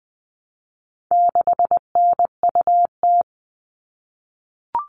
Morse code, heard as a keyed tone, about 20 wpm, 700 Hz. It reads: 6NUT
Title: All Cut Numbers - 4-Digit 20wpm